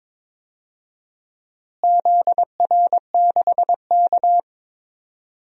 Morse code, heard as a keyed tone, about 22 wpm, 700 Hz. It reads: ZR6K
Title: DXCC-100 Call Signs - 4 Characters - Repeated 3x - 22wpm